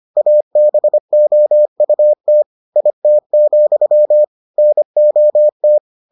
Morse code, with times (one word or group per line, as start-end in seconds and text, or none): 0.2-2.4 ABOUT
2.8-4.2 IT,
4.6-5.8 NOT